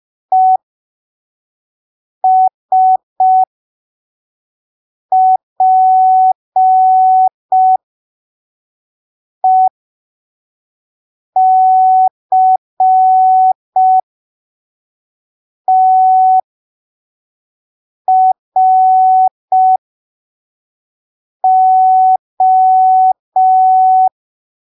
Morse code, heard as a keyed tone, about 5 wpm, spaced as for 3 wpm, 750 Hz. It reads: ESPECTRO
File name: arquivos / texto / texto 12 05ppm